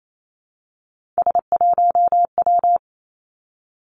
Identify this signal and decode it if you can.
Morse code, heard as a keyed tone, about 28 wpm, 700 Hz. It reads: S1W
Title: Three Challenging Letter-Number Combinations 28wpm